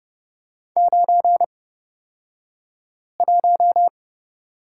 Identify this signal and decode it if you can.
Morse code, heard as a keyed tone, about 29 wpm, spaced as for 5 wpm, 700 Hz. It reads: 91
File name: Lesson 38d - Letter q - Two Characters 30wpm